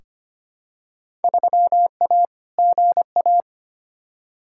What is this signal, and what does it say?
Morse code, heard as a keyed tone, about 25 wpm, 700 Hz. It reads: 3A GA
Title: Field Day Exchanges - 25wpm